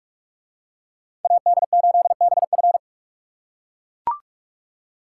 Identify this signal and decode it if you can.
Morse code, heard as a keyed tone, about 40 wpm, 700 Hz. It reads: AD8BF